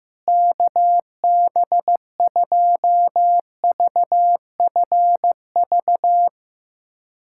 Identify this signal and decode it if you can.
Morse code, heard as a keyed tone, about 15 wpm, 700 Hz. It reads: KB2VFV
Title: USA Callsign DE Callsign - 6 Characters - 15wpm